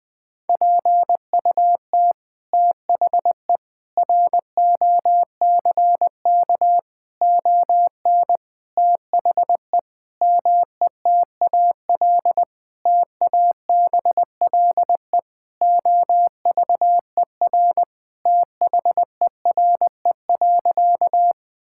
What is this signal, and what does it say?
Morse code, heard as a keyed tone, about 20 wpm, 700 Hz. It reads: PUT THE ROCK ON THE METAL TABLE OVER THERE.
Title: Sentences from Top 1000 Words - Review Part 6 - 20wpm